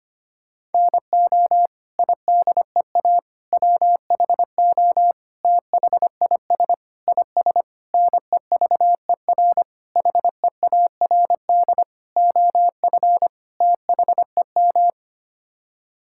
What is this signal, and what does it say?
Morse code, heard as a keyed tone, about 25 wpm, 700 Hz. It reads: NO IDEA WHO THIS IS NEVER HEARD OF THEM
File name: Sets of 9 Words - 25wpm